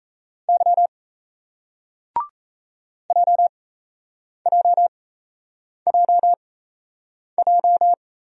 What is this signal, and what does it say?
Morse code, heard as a keyed tone, about 38 wpm, spaced as for 8 wpm, 700 Hz. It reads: Y JJJJ